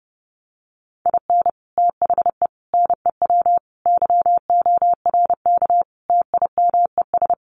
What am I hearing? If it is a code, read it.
Morse code, heard as a keyed tone, about 30 wpm, 700 Hz. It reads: IN THE NEW YORK TIMES
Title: Sets of 5 Words 30wpm